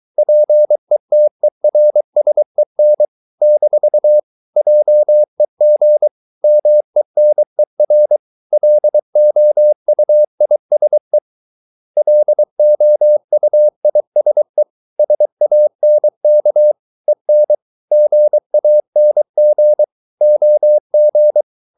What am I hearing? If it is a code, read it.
Morse code, heard as a keyed tone, about 23 wpm, 600 Hz. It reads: PETERSEN - JEG MENER LOUISE LOUISE SANK EN GANG OG
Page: Ferie 23 wpm